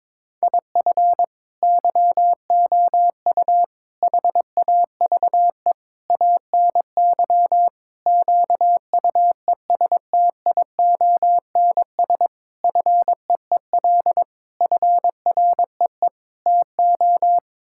Morse code, heard as a keyed tone, about 21 wpm, 700 Hz. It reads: IF YOU HAVE ANY QUESTIONS FEEL FREE TO
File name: practice / 22wpm / Sets of 8 Words - 22wpm